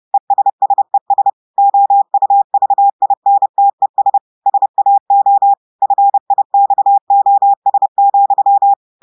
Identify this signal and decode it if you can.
Morse code, heard as a keyed tone, about 30 wpm, 800 Hz. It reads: ESSES OUVINTES SAO FIXOS,
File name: audios / portugais / portugais texte en clair 2 vitesse 30 mots minutes